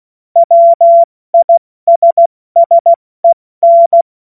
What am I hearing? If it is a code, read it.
Morse code, heard as a keyed tone, about 16 wpm, 680 Hz. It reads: WISSEN
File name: Morsekurs / Witze / Witz 21. Februar 2023 70 F80